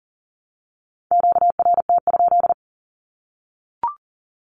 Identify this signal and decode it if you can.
Morse code, heard as a keyed tone, about 40 wpm, 700 Hz. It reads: QRT?
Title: Q-Codes 40wpm